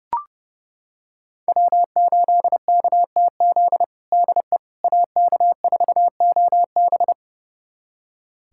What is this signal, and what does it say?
Morse code, heard as a keyed tone, about 30 wpm, 700 Hz. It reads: W8KTZ DE AK4OB
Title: USA Callsign DE Callsign - 5 Characters - 30wpm